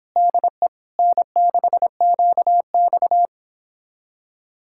Morse code, heard as a keyed tone, about 29 wpm, 700 Hz. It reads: DE N6QX